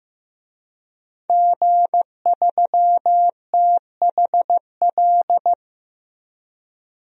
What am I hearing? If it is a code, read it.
Morse code, heard as a keyed tone, about 15 wpm, 700 Hz. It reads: G3THL
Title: DXCC-100 Call Signs - 5 Characters - 15wpm